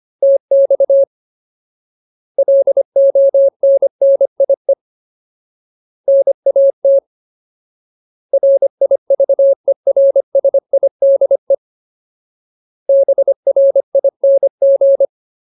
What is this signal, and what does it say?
Morse code, heard as a keyed tone, about 25 wpm, 550 Hz. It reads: TX LONNIE NAT RIVERSIDE BRING
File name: Ultimate Word List / 25WPM Ultimate